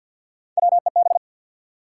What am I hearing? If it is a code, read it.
Morse code, heard as a keyed tone, about 50 wpm, 700 Hz. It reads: WEB
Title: Sentences from Top 1000 Words - Review Part 4 - New Word Drill - 50wpm